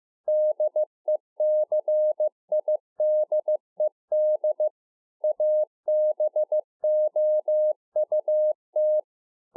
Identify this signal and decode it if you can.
Morse code, heard as a keyed tone, about 15 wpm, 610 Hz. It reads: DECIDED ABOUT